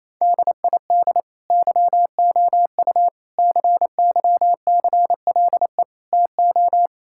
Morse code, heard as a keyed tone, about 27 wpm, 700 Hz. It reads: DID YOU CYCLE TO
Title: Sentences from Top 1000 Words - Review Part 6 - 28wpm